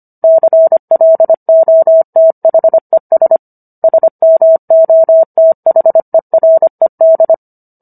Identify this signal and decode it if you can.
Morse code, heard as a keyed tone, about 25 wpm, 650 Hz. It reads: CLOTHES SMOTHERED